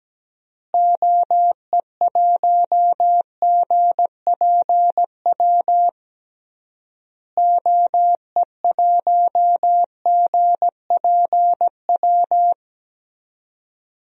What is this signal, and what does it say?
Morse code, heard as a keyed tone, about 17 wpm, 700 Hz. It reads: OE1GPW OE1GPW
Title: DXCC-100 Call Signs - 6 Characters - Repeated 3x - 17wpm